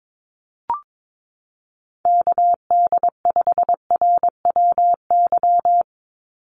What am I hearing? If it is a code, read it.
Morse code, heard as a keyed tone, about 22 wpm, 700 Hz. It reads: KD5RWY
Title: USA Call Signs 22wpm